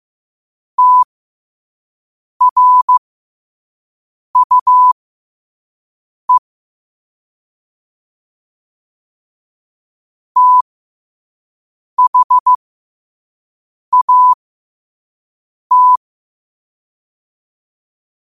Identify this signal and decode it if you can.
Morse code, heard as a keyed tone, about 15 wpm, spaced as for 5 wpm, 1000 Hz. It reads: TRUE THAT
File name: qotdfiles / 5wpm